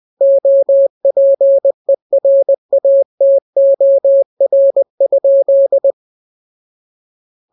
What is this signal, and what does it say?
Morse code, heard as a keyed tone, about 20 wpm, 550 Hz. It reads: OPERATOR?